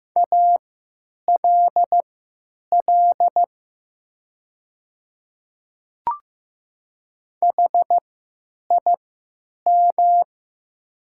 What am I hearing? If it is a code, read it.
Morse code, heard as a keyed tone, about 15 wpm, spaced as for 9 wpm, 700 Hz. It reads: ALL HIM
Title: 3x Character Spacing - Top 100 Words - 15wpm